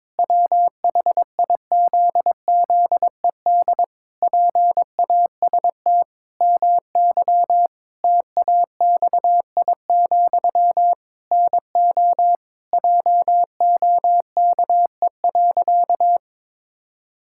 Morse code, heard as a keyed tone, about 22 wpm, 700 Hz. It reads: WHIZZED PAST MY TAXI, NO JOKE.
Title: Pangram Challenge v2 - 22wpm